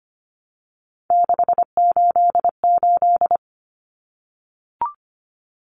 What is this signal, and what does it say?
Morse code, heard as a keyed tone, about 25 wpm, 700 Hz. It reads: 688
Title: Numbers - 3-Digit 25wpm